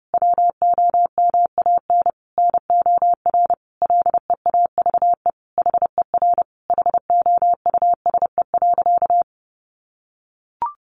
Morse code, heard as a keyed tone, about 30 wpm, 700 Hz. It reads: WOMAN NOR LEAVE HER HOUSE.
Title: Sentences from Top 1500 Words 30wpm